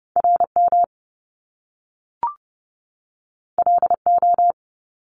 Morse code, heard as a keyed tone, about 30 wpm, 700 Hz. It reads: RM LO
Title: Top Two Letter Combinations in Words 30wpm